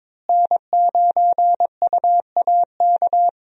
Morse code, heard as a keyed tone, about 22 wpm, 700 Hz. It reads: N9UAK
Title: USA Call Signs - 5 Characters - Repeated 2x - 22wpm